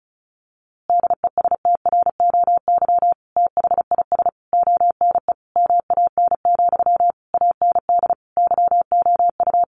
Morse code, heard as a keyed tone, about 34 wpm, 700 Hz. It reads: DESTROY THIS ONE MAN, AND YOU
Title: A Sound Of Thunder 35wpm